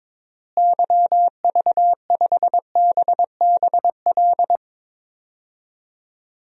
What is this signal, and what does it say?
Morse code, heard as a keyed tone, about 22 wpm, 700 Hz. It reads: YV5BBL
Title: DXCC-100 Call Signs - 6 Characters - Repeated 3x - 22wpm